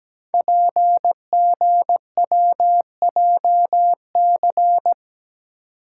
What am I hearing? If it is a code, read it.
Morse code, heard as a keyed tone, about 17 wpm, 700 Hz. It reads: PGWJC
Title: Code Groups - 5 Letters - 17wpm